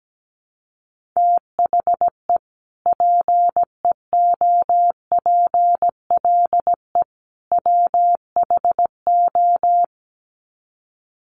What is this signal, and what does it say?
Morse code, heard as a keyed tone, about 17 wpm, 700 Hz. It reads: THE PEOPLE WHO
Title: Sets of 3 Words 17wpm